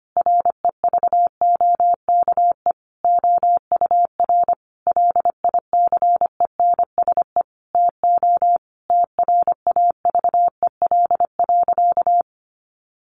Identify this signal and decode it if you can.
Morse code, heard as a keyed tone, about 25 wpm, 700 Hz. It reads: REVOKE OUR LICENSE TO TRAVEL.
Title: A Sound Of Thunder 25wpm